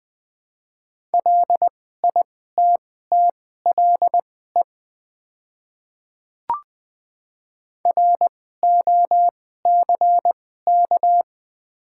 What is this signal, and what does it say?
Morse code, heard as a keyed tone, about 20 wpm, spaced as for 15 wpm, 700 Hz. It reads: LITTLE ROCK
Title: 2x Character Spacing - Top 500 Words - 20wpm